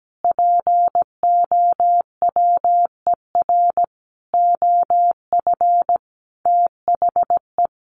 Morse code, heard as a keyed tone, about 17 wpm, 700 Hz. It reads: POWER OF THE
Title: Sentences from Top 1400 Words 17wpm